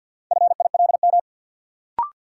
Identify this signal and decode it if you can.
Morse code, heard as a keyed tone, about 53 wpm, 700 Hz. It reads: FILM
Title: Sentences from Top 1000 Words - Review Part 3 - New Word Drill - 50wpm